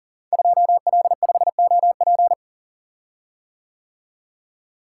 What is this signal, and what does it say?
Morse code, heard as a keyed tone, about 38 wpm, 700 Hz. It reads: 2L5OP
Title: Code Groups - 5 Letter:Numbers - 40wpm